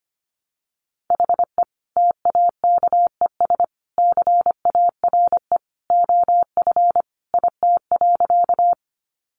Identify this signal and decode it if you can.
Morse code, heard as a keyed tone, about 25 wpm, 700 Hz. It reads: HE TAKES CARE OF IT.
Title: Additional Easy Phrases 25wpm